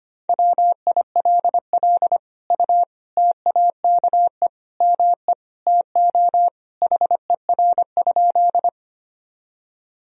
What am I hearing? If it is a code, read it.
Morse code, heard as a keyed tone, about 25 wpm, 700 Hz. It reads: WILL U TAKE ME TO HER?
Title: Sentences from Top 200 Words and Abbreviations 25wpm